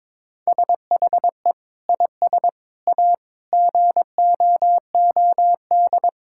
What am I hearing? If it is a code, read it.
Morse code, heard as a keyed tone, about 22 wpm, 700 Hz. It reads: SHE IS A GOOD
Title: Sentences from Top 1600 Words 22wpm